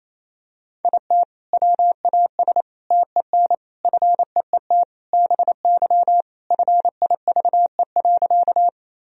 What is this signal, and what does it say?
Morse code, heard as a keyed tone, about 28 wpm, 700 Hz. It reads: IT WAS TEN FEET BY FIVE.
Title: Sentences from Top 1000 Words - Review Part 2 - 28wpm